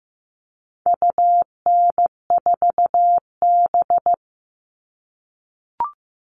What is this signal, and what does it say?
Morse code, heard as a keyed tone, about 15 wpm, 700 Hz. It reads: UN4B
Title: All Cut Numbers - 4-Digit 15wpm